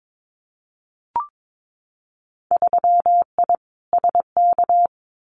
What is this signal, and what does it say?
Morse code, heard as a keyed tone, about 22 wpm, 700 Hz. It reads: 3I SK
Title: Winter Field Day Exchanges Encore 22wpm